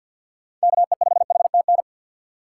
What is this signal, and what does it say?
Morse code, heard as a keyed tone, about 52 wpm, 700 Hz. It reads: KE5HTN
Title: USA Call Signs - 6 Characters - 3x Character Spacing - 50wpm